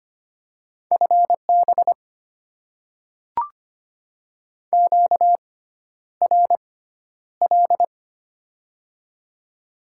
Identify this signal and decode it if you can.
Morse code, heard as a keyed tone, about 25 wpm, spaced as for 11 wpm, 700 Hz. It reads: FB QRL
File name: Top 90 QSO Elements - 6x Character Spacing - 25wpm